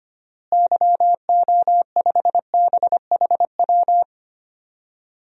Morse code, heard as a keyed tone, about 25 wpm, 700 Hz. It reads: YO5BHW
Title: DXCC-100 Call Signs - 6 Characters - Repeated 2x - 25wpm